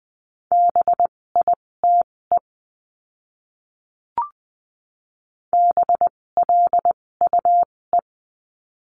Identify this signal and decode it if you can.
Morse code, heard as a keyed tone, about 20 wpm, spaced as for 16 wpm, 700 Hz. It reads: BITE BLUE
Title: 256 Lesson 42a - Words 12-20wpm